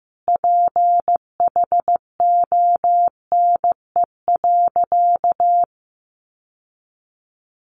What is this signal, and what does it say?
Morse code, heard as a keyed tone, about 15 wpm, 700 Hz. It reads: PHONE.
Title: Sentences from Top 1100 Words 15wpm